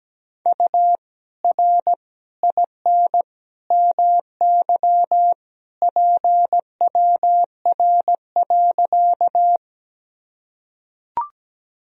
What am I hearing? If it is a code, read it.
Morse code, heard as a keyed tone, about 17 wpm, 700 Hz. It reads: U R IN MY PWR.
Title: Sentences from Top 500 Words and Abbreviations 17wpm